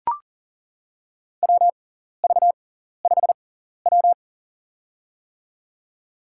Morse code, heard as a keyed tone, about 40 wpm, spaced as for 14 wpm, 700 Hz. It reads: WV5W
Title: USA Call Signs - 4 Characters - 6x Character Spacing - 40wpm